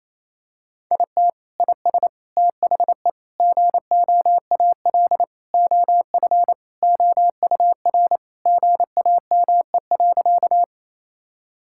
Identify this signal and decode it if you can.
Morse code, heard as a keyed tone, about 28 wpm, 700 Hz. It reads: IT IS THE GOAL OF OUR GAME.
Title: Sentences from Top 1000 Words - Review Part 5 - 28wpm